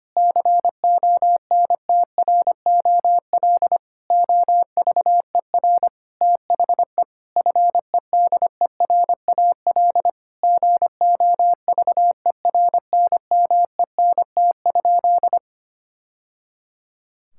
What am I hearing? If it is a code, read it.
Morse code, heard as a keyed tone, about 25 wpm, 700 Hz. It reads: CONTROL OVER THE FEDERAL GOVERNMENT?
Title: Sentences from Top 1000 Words - Review Part 5 - 25wpm